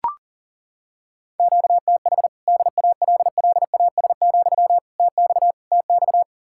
Morse code, heard as a keyed tone, about 40 wpm, 700 Hz. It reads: QTH DALLAS, TX TX
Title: Practice QSO - Exchange 1 - Beginning - 40wpm